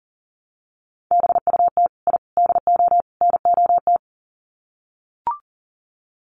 Morse code, heard as a keyed tone, about 40 wpm, 700 Hz. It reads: BUT I DO NOT
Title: Sets of 4 Words 40wpm